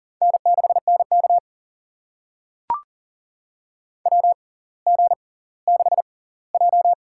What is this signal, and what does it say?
Morse code, heard as a keyed tone, about 40 wpm, spaced as for 21 wpm, 700 Hz. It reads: N6NK WG6J
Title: USA Call Signs - 4 Characters - 6x Character Spacing - 40wpm